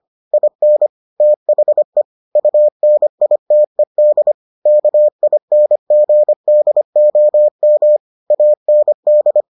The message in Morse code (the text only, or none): IN THE UNITED KINGDOM AND